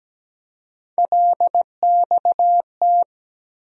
Morse code, heard as a keyed tone, about 17 wpm, 700 Hz. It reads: LXT
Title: USA Call Sign Postfixes - 3 Letters 17wpm